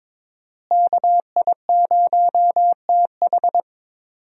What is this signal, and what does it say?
Morse code, heard as a keyed tone, about 22 wpm, 700 Hz. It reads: KI0TH
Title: USA Call Signs - 5 Characters - Repeated 3x - 22wpm